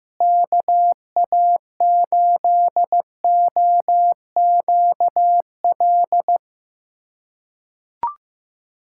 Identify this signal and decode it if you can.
Morse code, heard as a keyed tone, about 15 wpm, 700 Hz. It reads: KA8OQL
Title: USA Callsign DE Callsign 15wpm